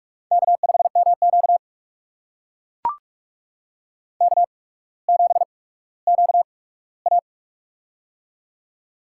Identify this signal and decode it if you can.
Morse code, heard as a keyed tone, about 44 wpm, spaced as for 19 wpm, 690 Hz. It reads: K5MQ K7QA
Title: USA Call Signs - 4 Characters - 8x Character Spacing - 45wpm